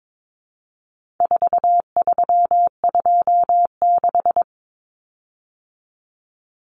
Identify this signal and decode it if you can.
Morse code, heard as a keyed tone, about 22 wpm, 700 Hz. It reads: 4326
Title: Common Cut Numbers - 4-Digit 22wpm